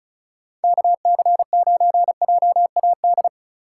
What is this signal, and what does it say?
Morse code, heard as a keyed tone, about 36 wpm, 700 Hz. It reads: KC9JAD
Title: USA Call Signs - 2x Character Spacing - 35wpm